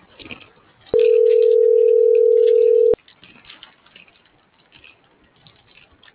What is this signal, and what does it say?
Ringback tone (North America), 440 + 480 Hz.